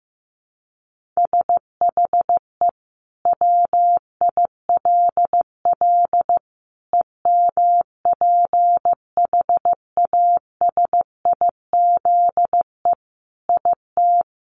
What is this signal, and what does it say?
Morse code, heard as a keyed tone, about 15 wpm, 700 Hz. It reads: SHE WILL EMPHASIZE IT
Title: Sentences from Top 1500 Words 15wpm